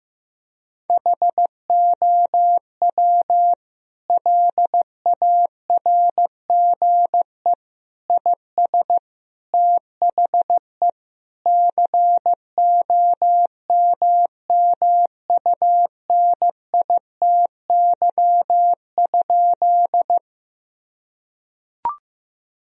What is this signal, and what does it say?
Morse code, heard as a keyed tone, about 16 wpm, 700 Hz. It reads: HOW LARGE IS THE COMMUNITY?
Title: Sentences from Top 1000 Words - Review Part 4 - 15wpm